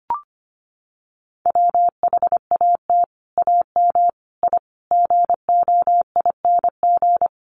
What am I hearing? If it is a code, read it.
Morse code, heard as a keyed tone, about 25 wpm, 700 Hz. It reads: WHAT AM I GOING
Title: Sets of 5 Words Encore 25wpm